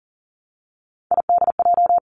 Morse code, heard as a keyed tone, about 41 wpm, 700 Hz. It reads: IDJ
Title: Three Challenging Letter-Number Combinations 40wpm